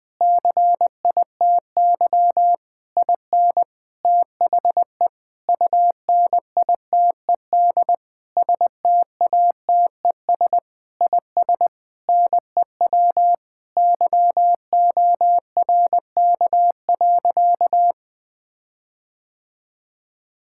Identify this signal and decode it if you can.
Morse code, heard as a keyed tone, about 20 wpm, 700 Hz. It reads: CITY IN THE UNITED STATES IS NEW YORK.
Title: Sentences with 5-Word Phrases - 20wpm